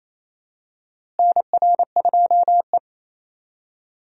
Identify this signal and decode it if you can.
Morse code, heard as a keyed tone, about 28 wpm, 700 Hz. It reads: NR2E
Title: USA Call Signs - 4 Characters - Repeated 3x - 28wpm